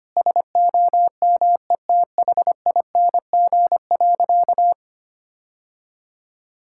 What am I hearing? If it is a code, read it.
Morse code, heard as a keyed tone, about 25 wpm, 700 Hz. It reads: SOMETHING.
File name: practice / 25wpm / Sentences with 4-Word Phrases - 25wpm